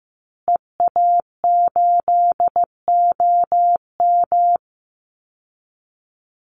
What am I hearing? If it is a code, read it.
Morse code, heard as a keyed tone, about 14 wpm, 700 Hz. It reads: EA8OM
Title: Call Signs Repeated 2x - 15wpm